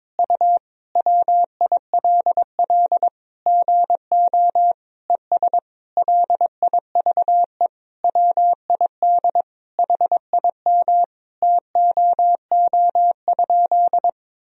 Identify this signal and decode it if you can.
Morse code, heard as a keyed tone, about 22 wpm, 700 Hz. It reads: U WILL GO ES LIVE WID HIM TOO?
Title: Sentences from Top 500 Words and Abbreviations 22wpm